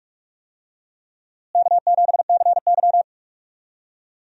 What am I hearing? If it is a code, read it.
Morse code, heard as a keyed tone, about 44 wpm, 700 Hz. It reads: K7CY
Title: USA Call Signs - 4 Characters - Repeated 3x - 45wpm